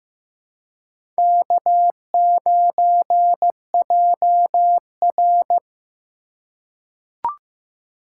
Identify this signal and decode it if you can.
Morse code, heard as a keyed tone, about 15 wpm, 700 Hz. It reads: K9JR
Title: USA Call Signs - 4 Characters - 3x Character Spacing - 15wpm